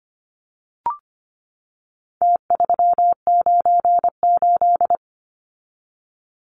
Morse code, heard as a keyed tone, about 25 wpm, 700 Hz. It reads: T398
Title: Numbers - Contest Style - 4-Digit 25wpm